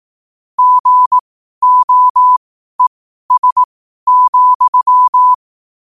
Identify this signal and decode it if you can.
Morse code, heard as a keyed tone, about 18 wpm, spaced as for 12 wpm, 1000 Hz. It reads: GOES,
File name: qotdfiles / 12wpm